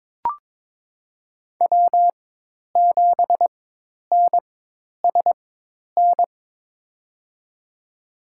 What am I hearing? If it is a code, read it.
Morse code, heard as a keyed tone, about 21 wpm, spaced as for 10 wpm, 700 Hz. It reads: W7NSN
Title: USA Call Signs - 5 Characters - 4x Character Spacing - 22wpm